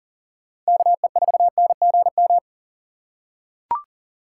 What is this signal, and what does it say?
Morse code, heard as a keyed tone, about 40 wpm, 690 Hz. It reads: KE4NGM
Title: USA Call Signs - 4x Character Spacing - 40wpm